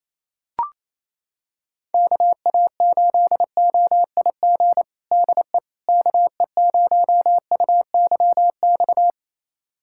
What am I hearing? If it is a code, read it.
Morse code, heard as a keyed tone, about 28 wpm, 700 Hz. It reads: KA8OIG DE KE0UYX
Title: USA Callsign DE Callsign - 6 Characters - 28wpm